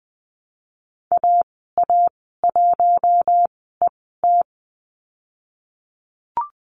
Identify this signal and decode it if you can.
Morse code, heard as a keyed tone, about 20 wpm, spaced as for 14 wpm, 700 Hz. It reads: AA1ET
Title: 055 Lesson 10g - Letter r - Call Signs 20wpm